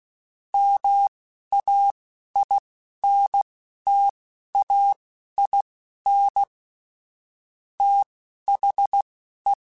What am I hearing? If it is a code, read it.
Morse code, heard as a keyed tone, about 16 wpm, spaced as for 11 wpm, 800 Hz. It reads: MAINTAIN THE